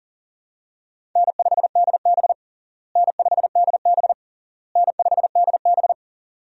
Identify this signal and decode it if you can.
Morse code, heard as a keyed tone, about 37 wpm, 700 Hz. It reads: N5DB N5DB N5DB
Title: USA Call Signs - 4 Characters - Repeated 3x - 40wpm